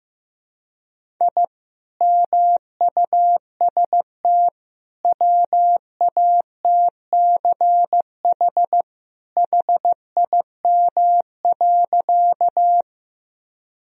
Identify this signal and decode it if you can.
Morse code, heard as a keyed tone, about 15 wpm, 700 Hz. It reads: I MUST WATCH HIM.
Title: Sentences from Top 1000 Words - Review Part 1 - 15wpm